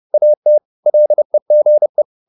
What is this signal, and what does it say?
Morse code, heard as a keyed tone, about 30 wpm, 600 Hz. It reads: AT LEGE
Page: Ferie 30 wpm